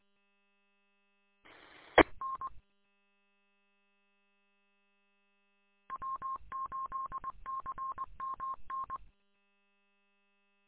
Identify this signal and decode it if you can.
Morse code, heard as a keyed tone, about 20 wpm, 1070 Hz. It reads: EN W8CMN
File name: w8cmn- -combined